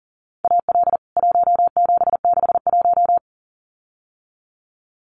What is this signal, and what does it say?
Morse code, heard as a keyed tone, about 40 wpm, 700 Hz. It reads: AL 1761
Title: CWT Exchange 40wpm